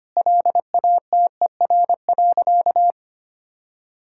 Morse code, heard as a keyed tone, about 25 wpm, 700 Hz. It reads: LATER.
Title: Sentences from Top 1000 Words - Review Part 6 - 25wpm